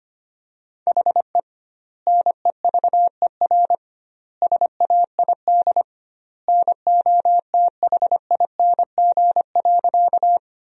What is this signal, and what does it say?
Morse code, heard as a keyed tone, about 25 wpm, 700 Hz. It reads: HE NEVER SAID NOTHING.
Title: 2x Word-Spacing - Sentences from Top 500 Words - 25wpm